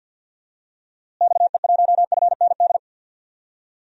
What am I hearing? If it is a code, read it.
Morse code, heard as a keyed tone, about 46 wpm, 700 Hz. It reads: XE1FND
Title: DXCC-100 Call Signs - 6 Characters - Repeated 3x - 50wpm